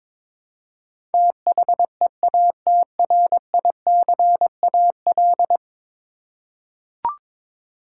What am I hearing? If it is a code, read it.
Morse code, heard as a keyed tone, about 21 wpm, 700 Hz. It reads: THEATRICAL
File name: practice / 22wpm / 10 Letter Words - First 3 Letters Given - 22wpm